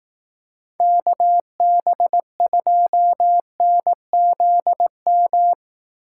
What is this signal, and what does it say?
Morse code, heard as a keyed tone, about 18 wpm, 700 Hz. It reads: KB2NZM
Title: Speed Racing - USA 2x3 Call Signs 15wpm